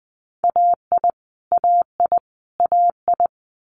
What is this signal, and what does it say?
Morse code, heard as a keyed tone, about 20 wpm, 700 Hz. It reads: AI AI AI
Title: Multi-letter Phonemes Repeated 3x 20wpm